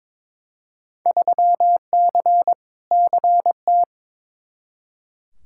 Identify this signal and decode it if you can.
Morse code, heard as a keyed tone, about 21 wpm, 700 Hz. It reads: 3C CT
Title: Field Day Exchanges Encore - 22wpm